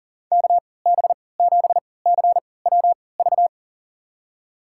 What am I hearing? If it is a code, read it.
Morse code, heard as a keyed tone, about 37 wpm, spaced as for 22 wpm, 700 Hz. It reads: KB7CWV
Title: USA Call Signs - 6 Characters - 3x Character Spacing - 40wpm